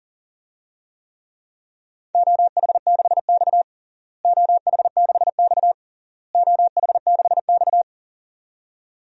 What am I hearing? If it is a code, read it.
Morse code, heard as a keyed tone, about 40 wpm, 700 Hz. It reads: OH6X OH6X OH6X
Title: DXCC-100 Call Signs - 4 Characters - Repeated 3x - 40wpm